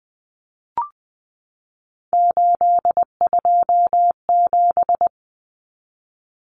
Morse code, heard as a keyed tone, about 20 wpm, 700 Hz. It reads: 827
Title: Common Cut Numbers - 3-Digit 20wpm